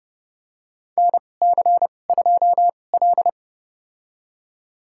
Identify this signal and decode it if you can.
Morse code, heard as a keyed tone, about 32 wpm, spaced as for 22 wpm, 700 Hz. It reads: NC2L
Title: USA Call Signs - 4 Characters - 2x Character Spacing - 30wpm